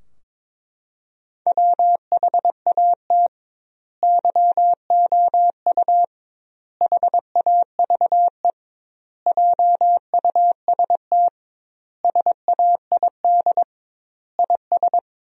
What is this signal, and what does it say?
Morse code, heard as a keyed tone, about 22 wpm, 700 Hz. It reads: WHAT YOU HAVE JUST SAID IS
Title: 2x Word-Spacing - Sentences from Top 200 Words - 22wpm